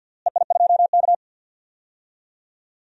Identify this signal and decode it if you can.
Morse code, heard as a keyed tone, about 50 wpm, 700 Hz. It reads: EI2K
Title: DXCC-100 Call Signs - 4 Characters - Repeated 3x - 50wpm